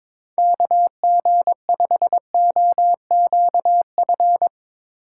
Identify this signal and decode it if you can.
Morse code, heard as a keyed tone, about 22 wpm, 700 Hz. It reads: KG5OQF
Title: USA Call Signs - 6 Characters - 3x Character Spacing - 22wpm